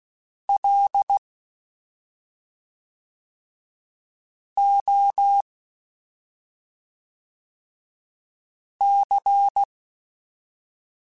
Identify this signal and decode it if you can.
Morse code, heard as a keyed tone, about 16 wpm, 800 Hz. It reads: L O C
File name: Special INGs / Special INGs-mp3s / SlowToFaster ing @5